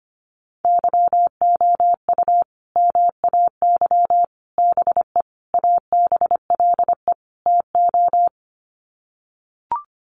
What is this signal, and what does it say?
Morse code, heard as a keyed tone, about 25 wpm, 700 Hz. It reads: YOU MAY BE ABLE TO
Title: Sets of 5 Words Encore 25wpm